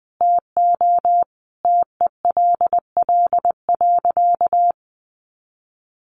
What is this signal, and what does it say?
Morse code, heard as a keyed tone, about 20 wpm, 700 Hz. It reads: TO TELL.
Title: Sentences from Top 300 Words 20wpm